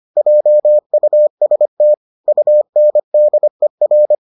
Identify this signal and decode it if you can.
Morse code, heard as a keyed tone, about 25 wpm, 600 Hz. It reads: JUST UNDER